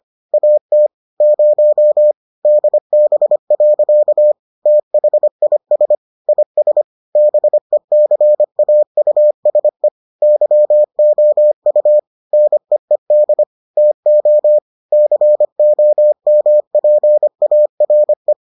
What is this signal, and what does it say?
Morse code, heard as a keyed tone, about 25 wpm, 600 Hz. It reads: AT 0 DB. THIS IS BECAUSE YOU NEED TO COMPARE